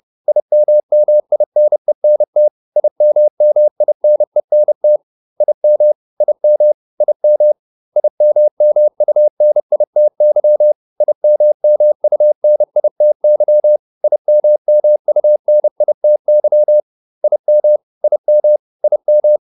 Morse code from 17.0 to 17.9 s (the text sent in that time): IM